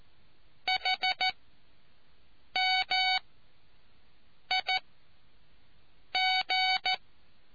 Morse code, heard as a keyed tone, about 14 wpm, spaced as for 5 wpm, 750 Hz. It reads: HMIG